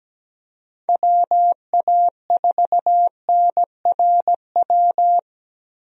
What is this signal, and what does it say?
Morse code, heard as a keyed tone, about 17 wpm, 700 Hz. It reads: WA4NRW
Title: USA Call Signs - 6 Characters - Repeated 3x - 17wpm